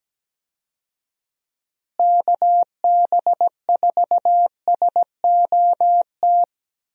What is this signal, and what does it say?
Morse code, heard as a keyed tone, about 17 wpm, 700 Hz. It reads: KB4SOT
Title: USA Call Signs - 6 Characters - Repeated 2x - 17wpm